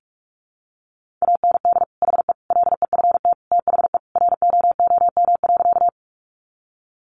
Morse code, heard as a keyed tone, about 44 wpm, 700 Hz. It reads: AND HE LEFT THE ROOM.
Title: Sentences from Top 300 Words 45wpm